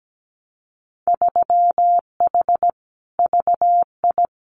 Morse code, heard as a keyed tone, about 17 wpm, 700 Hz. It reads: 3H VI
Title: Winter Field Day Exchanges Encore 17wpm